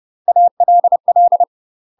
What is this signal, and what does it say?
Morse code, heard as a keyed tone, about 30 wpm, 700 Hz. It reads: ALL
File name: UK MASTER QSOs TRAINING / 30WPM / UK MASTER QSOs 0011